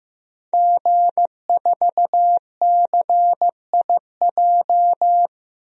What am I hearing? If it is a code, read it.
Morse code, heard as a keyed tone, about 15 wpm, 700 Hz. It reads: G4CIJ
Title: DXCC-100 Call Signs - 5 Characters - 15wpm